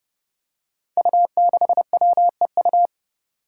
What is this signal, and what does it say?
Morse code, heard as a keyed tone, about 32 wpm, 700 Hz. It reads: U6WEU
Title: Five Challenging Letter-Number Combinations 30wpm